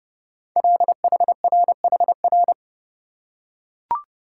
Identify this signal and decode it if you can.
Morse code, heard as a keyed tone, about 29 wpm, 700 Hz. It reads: LHRHR
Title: Code Groups - 5 Letter:Numbers - 30wpm